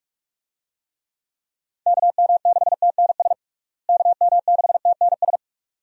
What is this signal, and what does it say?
Morse code, heard as a keyed tone, about 41 wpm, 700 Hz. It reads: KM6TNS KM6TNS
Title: USA Call Signs - 6 Characters - Repeated 2x - 45wpm